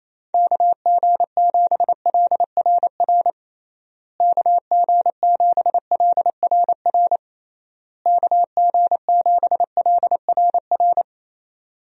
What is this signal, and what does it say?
Morse code, heard as a keyed tone, about 28 wpm, 700 Hz. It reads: KG7LRR KG7LRR KG7LRR
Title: USA Call Signs - 6 Characters - Repeated 3x - 28wpm